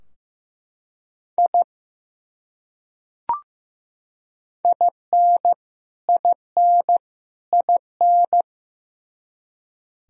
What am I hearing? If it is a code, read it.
Morse code, heard as a keyed tone, about 15 wpm, 700 Hz. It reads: I IN IN IN
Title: Top 100 Words Repeated 15wpm